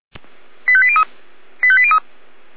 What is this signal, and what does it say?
Five-tone selective call (ZVEI), heard as quick runs of five tones, tones of about 70 ms. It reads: 85803 85803
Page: /455/SDR/ZVEI/